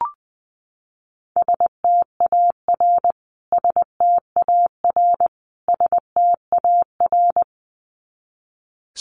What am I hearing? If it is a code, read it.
Morse code, heard as a keyed tone, about 20 wpm, 700 Hz. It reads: STAR STAR STAR
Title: Top 500 Words Repeated 20wpm